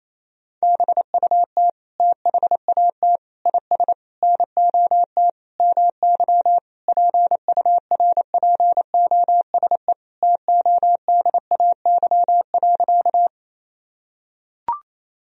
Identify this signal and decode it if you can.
Morse code, heard as a keyed tone, about 28 wpm, 700 Hz. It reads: BUT THAT IS NOT MY PURPOSE TODAY.
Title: Sentences from Top 1000 Words - Review Part 4 - 28wpm